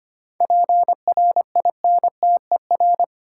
Morse code, heard as a keyed tone, about 25 wpm, 700 Hz. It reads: PRINTER
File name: practice / 25wpm / Easy Words with Hints - 25wpm